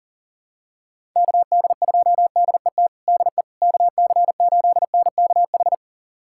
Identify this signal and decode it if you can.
Morse code, heard as a keyed tone, about 40 wpm, 700 Hz. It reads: KD2DET DE KC8NKH
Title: USA Callsign DE Callsign - 6 Characters - 40wpm